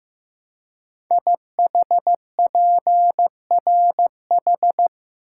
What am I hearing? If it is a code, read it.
Morse code, heard as a keyed tone, about 15 wpm, 700 Hz. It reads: IHPRH
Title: Code Groups - 5 Letters - 15wpm